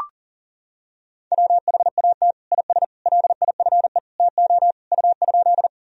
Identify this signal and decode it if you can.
Morse code, heard as a keyed tone, about 40 wpm, 700 Hz. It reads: WHAT IS LIFE TO U?
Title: Sentences from Top 300 Words and Abbreviations 40wpm